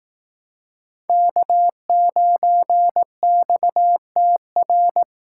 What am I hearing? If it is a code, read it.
Morse code, heard as a keyed tone, about 19 wpm, 700 Hz. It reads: K9XTR